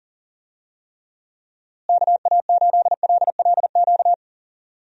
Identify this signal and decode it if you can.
Morse code, heard as a keyed tone, about 40 wpm, 700 Hz. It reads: KA8LLQ
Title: USA Call Signs - 6 Characters - Repeated 3x - 40wpm